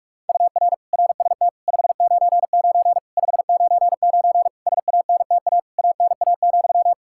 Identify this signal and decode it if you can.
Morse code, heard as a keyed tone, about 46 wpm, 700 Hz. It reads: UR RST 599 599 SANTA ANA,
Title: Practice QSO - Exchange 1 - Beginning - 45wpm